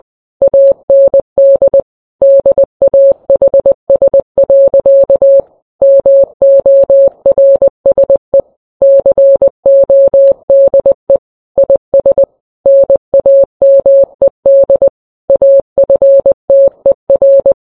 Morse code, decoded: AND DAHS. MORSE CODE IS NAMED AFTER